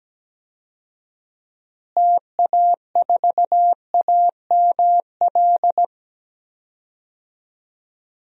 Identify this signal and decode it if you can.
Morse code, heard as a keyed tone, about 17 wpm, 700 Hz. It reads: TA4AML
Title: DXCC-100 Call Signs - 6 Characters - Repeated 3x - 17wpm